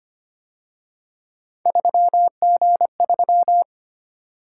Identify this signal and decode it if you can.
Morse code, heard as a keyed tone, about 29 wpm, 700 Hz. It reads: 3G3
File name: Speed-Racing - Three Letter-Number Combinations - 25wpm